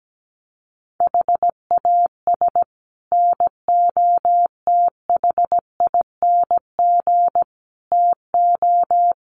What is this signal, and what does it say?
Morse code, heard as a keyed tone, about 17 wpm, 700 Hz. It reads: HAS NOTHING TO
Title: Sets of 5 Words 17wpm